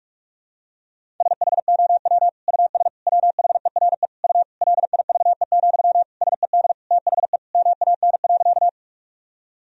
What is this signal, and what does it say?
Morse code, heard as a keyed tone, about 46 wpm, 700 Hz. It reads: SHOW US WHERE U LIVE, SED THE MAN.
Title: Sentences from Top 200 Words and Abbreviations 45wpm